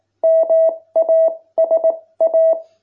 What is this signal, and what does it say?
Morse code, heard as a keyed tone, about 18 wpm, spaced as for 16 wpm, 660 Hz. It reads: MASA